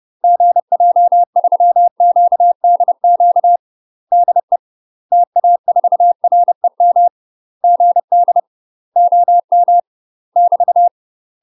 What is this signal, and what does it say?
Morse code, heard as a keyed tone, about 30 wpm, 700 Hz. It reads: GJ3QDQ DE TA4REM GD OM <BT>
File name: UK MASTER QSOs TRAINING / 30WPM / UK MASTER QSOs 0033